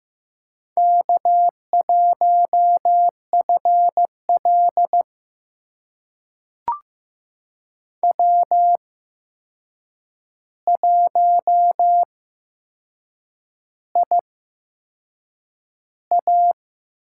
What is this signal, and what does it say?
Morse code, heard as a keyed tone, about 15 wpm, spaced as for 6 wpm, 700 Hz. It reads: K1FL W1IA
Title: USA Call Signs - 4 Characters - 8x Character Spacing - 15wpm